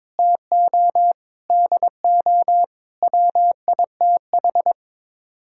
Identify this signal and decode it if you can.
Morse code, heard as a keyed tone, about 22 wpm, 700 Hz. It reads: TO DO WITH
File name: practice / 22wpm / Build A Phrase - 5-Word Phrases - 22wpm